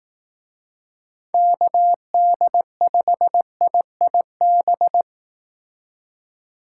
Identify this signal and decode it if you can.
Morse code, heard as a keyed tone, about 18 wpm, 700 Hz. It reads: KD5IIB
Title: Speed Racing - USA 2x3 Call Signs 15wpm